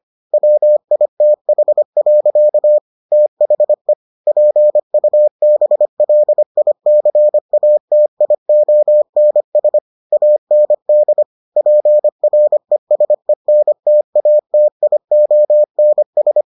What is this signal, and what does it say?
Morse code, heard as a keyed tone, about 25 wpm, 600 Hz. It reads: WITH. THE PUBLICATIONS AND PRESENTATIONS